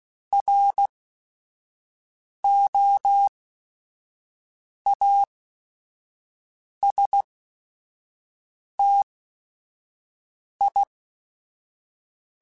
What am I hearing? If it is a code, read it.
Morse code, heard as a keyed tone, about 16 wpm, spaced as for 5 wpm, 800 Hz. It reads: ROASTI